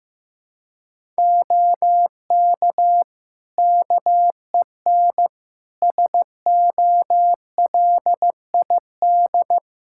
Morse code, heard as a keyed tone, about 15 wpm, 700 Hz. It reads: OK KEN SOLID
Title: Practice QSO - Exchange 3 - More Info - 15wpm